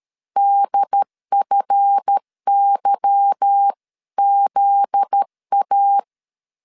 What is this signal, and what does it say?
Morse code, heard as a keyed tone, about 13 wpm, 800 Hz. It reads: DFYZA
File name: LEZIONI / 8 YJ45 / 8 Y 12wpm 2228